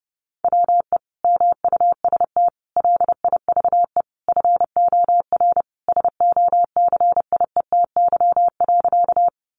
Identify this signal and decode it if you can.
Morse code, heard as a keyed tone, about 30 wpm, 700 Hz. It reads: WE MUST LIVE FOR SOCIETY.
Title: Sentences from Top 700 Words 30wpm